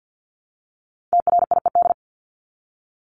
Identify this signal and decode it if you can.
Morse code, heard as a keyed tone, about 50 wpm, 700 Hz. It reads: TRIED
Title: ED Suffix - Top 100 Words with Base Words 50wpm